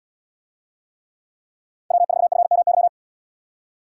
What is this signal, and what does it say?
Morse code, heard as a keyed tone, about 78 wpm, 700 Hz. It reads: W4FMY